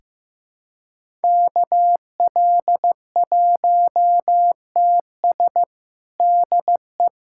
Morse code, heard as a keyed tone, about 15 wpm, 700 Hz. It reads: KL1TS DE
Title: USA Callsign DE Callsign 15wpm